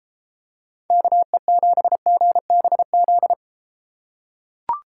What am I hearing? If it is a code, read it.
Morse code, heard as a keyed tone, about 31 wpm, 700 Hz. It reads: KE7GBZ